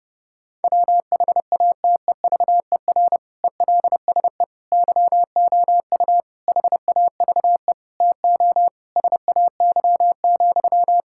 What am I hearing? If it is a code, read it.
Morse code, heard as a keyed tone, about 29 wpm, 700 Hz. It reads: WHATEVER ELSE YOU HAVE TO SAY,
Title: Sentences from Top 1600 Words 30wpm